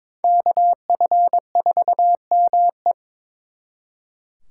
Morse code, heard as a keyed tone, about 26 wpm, 700 Hz. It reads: KF4ME
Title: Speed Racing - USA 2x2 Call Signs 22wpm